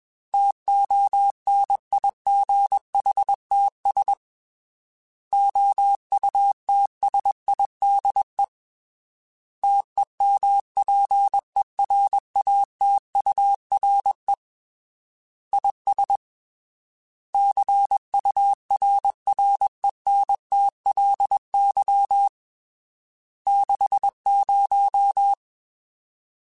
Morse code, heard as a keyed tone, about 21 wpm, 800 Hz. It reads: TONIGHTS OUTSIDE TEMPERATURE IS CURRENTLY 60
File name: Weather Sets / Weather Sets-mp3s / Weather 5 22 @20